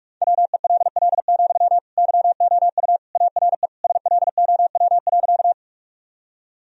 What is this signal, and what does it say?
Morse code, heard as a keyed tone, about 41 wpm, 700 Hz. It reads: WELL, YOU ARE SLOW.
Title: Sentences from Top 1000 Words - Review Part 2 - 45wpm